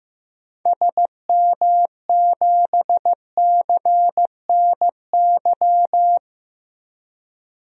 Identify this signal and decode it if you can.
Morse code, heard as a keyed tone, about 15 wpm, 700 Hz. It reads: SM7CNY
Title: DXCC-100 Call Signs - 6 Characters - Repeated 3x - 15wpm